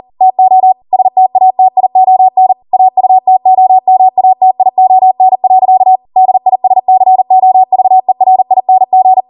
Morse code, heard as a keyed tone, about 40 wpm, 750 Hz. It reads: TO STATION AUTOMATION. DISCOVERING